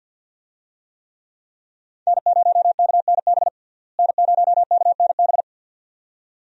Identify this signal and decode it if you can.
Morse code, heard as a keyed tone, about 52 wpm, 700 Hz. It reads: N0KNB N0KNB
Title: USA Call Signs - 5 Characters - Repeated 2x - 50wpm